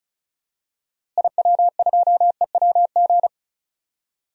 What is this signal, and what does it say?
Morse code, heard as a keyed tone, about 36 wpm, 700 Hz. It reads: IW2EWG
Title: DXCC-100 Call Signs - 6 Characters - Repeated 3x - 35wpm